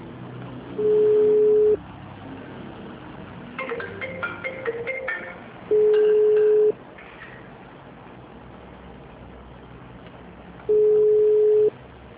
Ringback tone (Europe, 425 Hz).
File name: out- -2002- - - .5